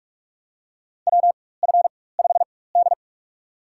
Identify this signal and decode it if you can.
Morse code, heard as a keyed tone, about 46 wpm, spaced as for 21 wpm, 700 Hz. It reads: WF5D
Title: USA Call Signs - 4 Characters - 4x Character Spacing - 45wpm